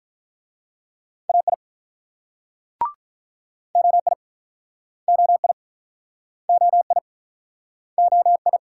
Morse code, heard as a keyed tone, about 48 wpm, 700 Hz. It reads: AI OI OI OI OI